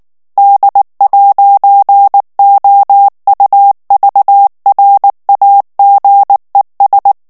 Morse code, heard as a keyed tone, about 19 wpm, 800 Hz. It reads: D'OUVRAGES
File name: FRANCAIS TEXTE CLAIR 2 VITESSE 18 MOTS MINUTES